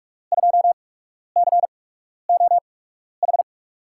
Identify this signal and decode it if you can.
Morse code, heard as a keyed tone, about 46 wpm, spaced as for 12 wpm, 700 Hz. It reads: 2COH
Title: USA Call Signs - 6 Characters - 8x Character Spacing - 45wpm